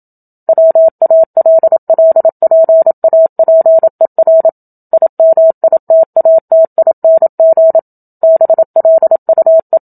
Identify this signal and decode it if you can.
Morse code, heard as a keyed tone, about 27 wpm, 650 Hz. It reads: WALLPAPER IMITATING BLUE
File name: Book Two / 01-Chapter 1 - 27 WPM